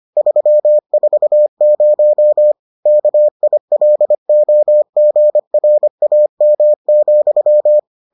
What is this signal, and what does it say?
Morse code, heard as a keyed tone, about 25 wpm, 600 Hz. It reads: 340 KILOGRAM,